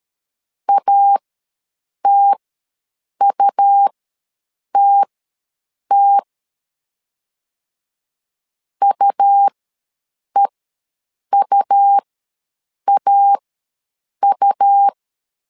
Morse code, heard as a keyed tone, about 13 wpm, spaced as for 7 wpm, 800 Hz. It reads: ATUTT UEUAU
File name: LEZIONI / 2 UHDN / 2 U ws 12wpm 1655